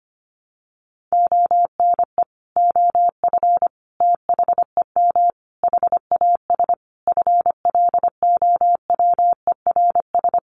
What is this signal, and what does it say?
Morse code, heard as a keyed tone, about 25 wpm, 700 Hz. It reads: ONE OF THEM HAS FLOWERS
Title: Additional Easy Phrases 25wpm